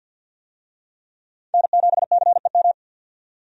Morse code, heard as a keyed tone, about 50 wpm, 700 Hz. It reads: N7CEM
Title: USA Call Signs - 5 Characters - 50wpm